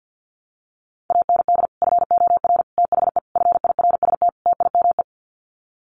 Morse code, heard as a keyed tone, about 50 wpm, 700 Hz. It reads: AND FOR THE FIRST TIME
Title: Sets of 5 Words 50wpm